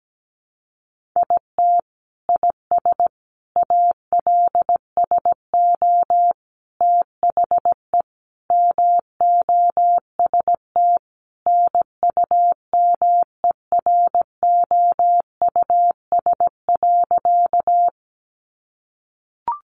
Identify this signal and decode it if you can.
Morse code, heard as a keyed tone, about 17 wpm, 700 Hz. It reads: IT IS ALSO THE MOST NUMEROUS.
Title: Sentences from Top 1400 Words 17wpm